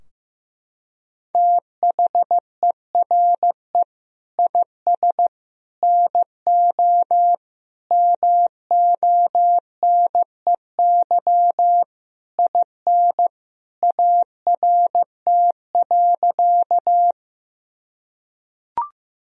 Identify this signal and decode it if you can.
Morse code, heard as a keyed tone, about 15 wpm, 700 Hz. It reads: THERE IS NO MONEY IN ART.
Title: Sentences from Top 1000 Words - Review Part 3 - 15wpm